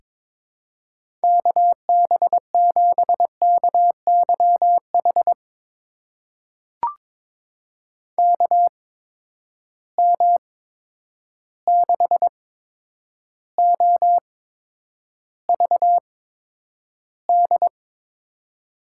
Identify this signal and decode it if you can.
Morse code, heard as a keyed tone, about 22 wpm, spaced as for 9 wpm, 700 Hz. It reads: KB7KYH KM6OVD
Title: USA Call Signs - 6 Characters - 8x Character Spacing - 22wpm